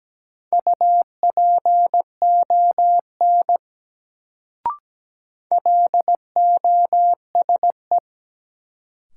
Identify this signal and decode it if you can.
Morse code, heard as a keyed tone, about 17 wpm, 700 Hz. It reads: UPON LOSE
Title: Sentences from Top 1100 Words - New Word Drill Alt - 17wpm